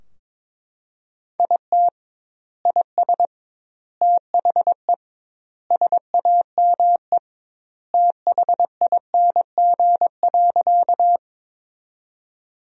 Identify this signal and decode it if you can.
Morse code, heard as a keyed tone, about 22 wpm, 700 Hz. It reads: IT IS THE SAME THING.
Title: 2x Word-Spacing - Sentences from Top 200 Words - 22wpm